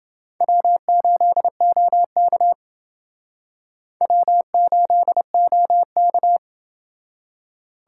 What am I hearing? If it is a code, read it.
Morse code, heard as a keyed tone, about 29 wpm, 700 Hz. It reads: W8OK W8OK